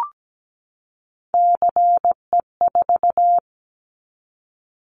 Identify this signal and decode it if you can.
Morse code, heard as a keyed tone, about 17 wpm, 700 Hz. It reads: CE4
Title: Callsign Prefixes for Top 100 DXCC - 17wpm